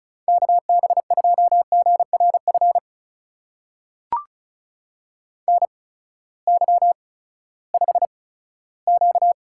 Morse code, heard as a keyed tone, about 36 wpm, spaced as for 18 wpm, 700 Hz. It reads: KB2GRF NY5Q
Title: USA Call Signs - 8x Character Spacing - 35wpm